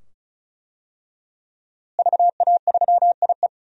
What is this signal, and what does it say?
Morse code, heard as a keyed tone, about 36 wpm, 700 Hz. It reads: VA3IE
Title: DXCC-100 Call Signs - 5 Characters - Repeated 2x - 35wpm